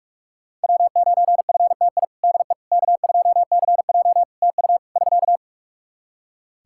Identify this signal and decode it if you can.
Morse code, heard as a keyed tone, about 43 wpm, 700 Hz. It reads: W9FTI DE K2CJ TU <SK>
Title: K1USN SST QSO Practice 45wpm